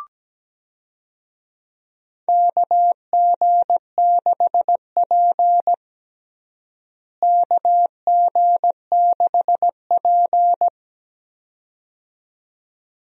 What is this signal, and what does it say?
Morse code, heard as a keyed tone, about 17 wpm, 700 Hz. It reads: KG6P KG6P
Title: USA Call Signs - 4 Characters - Repeated 2x - 17wpm